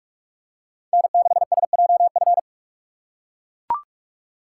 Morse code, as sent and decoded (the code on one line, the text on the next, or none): -. -.... ... .--- ..-.
N6SJF